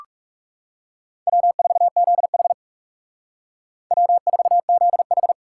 Morse code, heard as a keyed tone, about 36 wpm, 700 Hz. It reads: W4ZH W4ZH
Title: Speed Racing - USA 1x2 Call Signs 30wpm